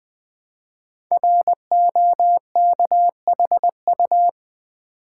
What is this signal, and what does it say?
Morse code, heard as a keyed tone, about 20 wpm, 700 Hz. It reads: ROKHU
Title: Code Groups - 5 Letters - 20wpm